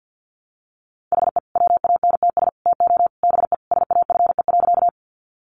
Morse code, heard as a keyed tone, about 50 wpm, 700 Hz. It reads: HE WANTS TO BE SAFE.
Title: Additional Easy Phrases 50wpm